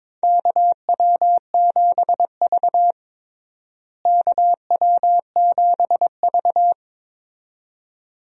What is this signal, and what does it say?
Morse code, heard as a keyed tone, about 22 wpm, 700 Hz. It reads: KW7V KW7V
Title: USA Call Signs - 4 Characters - Repeated 2x - 22wpm